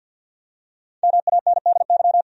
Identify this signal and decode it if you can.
Morse code, heard as a keyed tone, about 51 wpm, 700 Hz. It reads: MANDY
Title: USA Common Names - Max 5 Letters - 50wpm